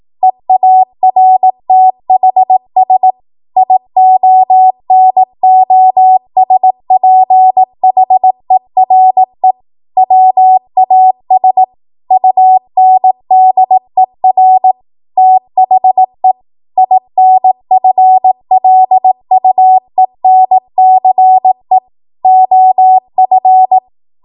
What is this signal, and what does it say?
Morse code, heard as a keyed tone, about 18 wpm, 750 Hz. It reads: EARTHS IONOSPHERE WAS UNDER THE INFLUENCE OF
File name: Morse / 18WPM